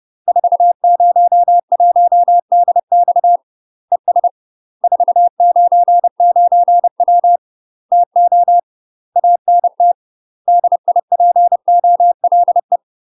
Morse code, heard as a keyed tone, about 30 wpm, 700 Hz. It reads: 401DX ES 499W TO ANT DIPOLE
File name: UK MASTER QSOs TRAINING / 30WPM / UK MASTER QSOs 0001